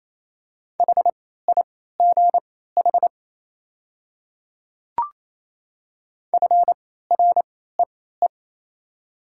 Morse code, heard as a keyed tone, about 28 wpm, spaced as for 16 wpm, 700 Hz. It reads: HIGH FREE
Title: 3x Character Spacing - Top 500 Words - 28wpm